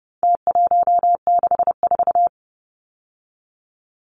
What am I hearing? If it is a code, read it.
Morse code, heard as a keyed tone, about 30 wpm, 700 Hz. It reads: T164
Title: Numbers - Contest Style - 4-Digit 30wpm